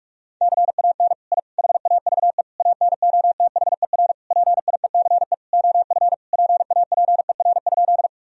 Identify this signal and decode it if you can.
Morse code, heard as a keyed tone, about 41 wpm, 700 Hz. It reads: CAN I HAVE ANOTHER PIECE OF PAPER?
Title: Sentences from Top 1000 Words - Review Part 2 - 45wpm